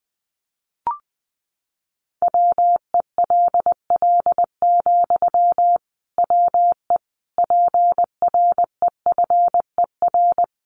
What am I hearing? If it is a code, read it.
Morse code, heard as a keyed tone, about 20 wpm, 700 Hz. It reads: WELL, WE PREFER
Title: Sentences from Top 1100 Words 20wpm